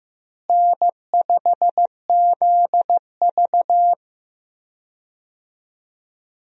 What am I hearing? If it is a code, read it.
Morse code, heard as a keyed tone, about 15 wpm, 700 Hz. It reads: N5ZV
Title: USA Call Signs - 4 Characters - Repeated 3x - 15wpm